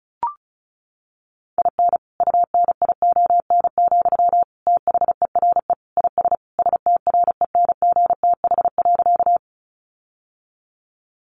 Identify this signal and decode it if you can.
Morse code, heard as a keyed tone, about 34 wpm, 700 Hz. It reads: IN UNION, THERE IS STRENGTH.
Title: Sentences from Top 1400 Words 35wpm